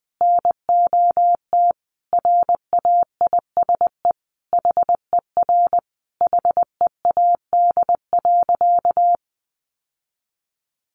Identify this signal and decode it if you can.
Morse code, heard as a keyed tone, about 20 wpm, 700 Hz. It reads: NOT RAISE HER HEAD.
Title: Sentences from Top 1400 Words 20wpm